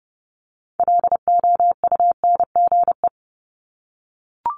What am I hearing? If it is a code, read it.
Morse code, heard as a keyed tone, about 30 wpm, 700 Hz. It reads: LOUNGE
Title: Multi-letter Phonemes 30wpm